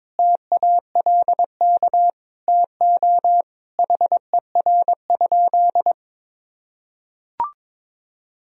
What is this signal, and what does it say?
Morse code, heard as a keyed tone, about 22 wpm, 700 Hz. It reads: TALK TO HER?
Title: Sentences from Top 1000 Words - Review Part 4 - 22wpm